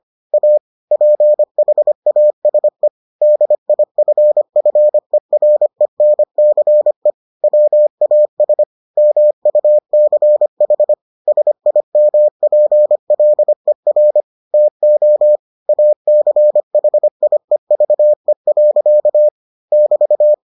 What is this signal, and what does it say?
Morse code, heard as a keyed tone, about 25 wpm, 600 Hz. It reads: A PHASE DIFFERENCE WAS MUCH SIMPLER TO ACHIEVE. <BT>